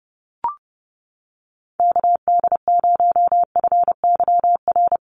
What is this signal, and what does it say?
Morse code, heard as a keyed tone, about 30 wpm, 700 Hz. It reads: KD0FYR
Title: USA Call Signs Repeated 2x 30wpm